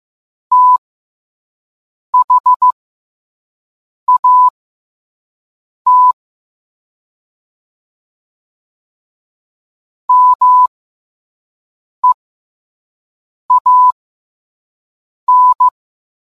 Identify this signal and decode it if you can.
Morse code, heard as a keyed tone, about 15 wpm, spaced as for 5 wpm, 1000 Hz. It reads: THAT MEAN